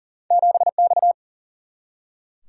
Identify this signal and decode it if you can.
Morse code, heard as a keyed tone, about 40 wpm, 700 Hz. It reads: 7 X
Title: Single-Letter Single-Number 40wpm